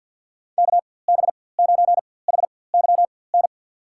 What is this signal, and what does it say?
Morse code, heard as a keyed tone, about 52 wpm, spaced as for 23 wpm, 700 Hz. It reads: KB9HYN
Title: USA Call Signs - 6 Characters - 4x Character Spacing - 50wpm